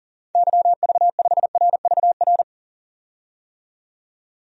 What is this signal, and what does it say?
Morse code, heard as a keyed tone, about 40 wpm, 700 Hz. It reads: YV5RVR
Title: DXCC-100 Call Signs - 6 Characters - 40wpm